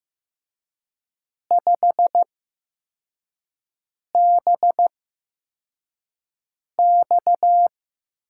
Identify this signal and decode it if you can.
Morse code, heard as a keyed tone, about 15 wpm, spaced as for 4 wpm, 700 Hz. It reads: 5BX
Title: USA Call Signs - 8x Character Spacing - 15wpm